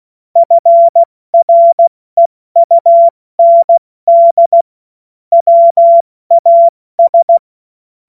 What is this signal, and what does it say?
Morse code, heard as a keyed tone, about 16 wpm, 680 Hz. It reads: FREUND WAS
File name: Morsekurs / Witze / Witz 07. September 2021, 70 F80